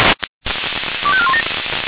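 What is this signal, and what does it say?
Five-tone selective call (ZVEI), heard as quick runs of five tones, tones of about 70 ms. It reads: 25417
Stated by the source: ZVEI five-tone selective call received by SDR.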